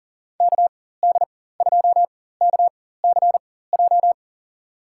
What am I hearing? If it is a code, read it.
Morse code, heard as a keyed tone, about 40 wpm, spaced as for 19 wpm, 700 Hz. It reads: KD2KCJ
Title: USA Call Signs - 6 Characters - 4x Character Spacing - 40wpm